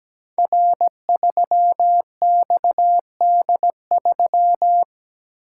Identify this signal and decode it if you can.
Morse code, heard as a keyed tone, about 17 wpm, 700 Hz. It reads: R3XD3
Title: Code Groups - 5 Letter:Numbers - 17wpm